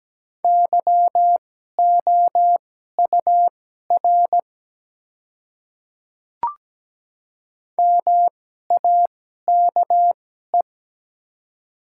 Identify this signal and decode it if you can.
Morse code, heard as a keyed tone, about 17 wpm, spaced as for 12 wpm, 700 Hz. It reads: YOUR MAKE
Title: 2x Character Spacing - Top 100 Words - 17wpm